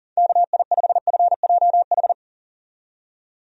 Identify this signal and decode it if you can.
Morse code, heard as a keyed tone, about 40 wpm, 700 Hz. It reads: KI5FJH
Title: USA Call Signs - 6 Characters - 40wpm